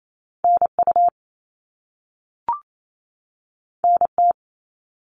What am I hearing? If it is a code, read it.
Morse code, heard as a keyed tone, about 28 wpm, 700 Hz. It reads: NU NT
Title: Canadian Provinces and Territories 28wpm